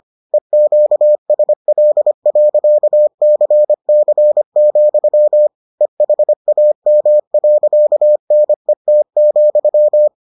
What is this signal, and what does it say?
Morse code, heard as a keyed tone, about 25 wpm, 600 Hz. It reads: EQSL.CC, EHAM.NET,